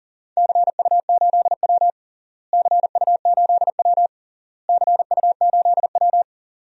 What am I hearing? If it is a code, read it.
Morse code, heard as a keyed tone, about 40 wpm, 700 Hz. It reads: CU8W CU8W CU8W
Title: DXCC-100 Call Signs - 4 Characters - Repeated 3x - 40wpm